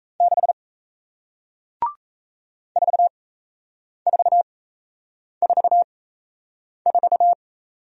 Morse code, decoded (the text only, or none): 6 4444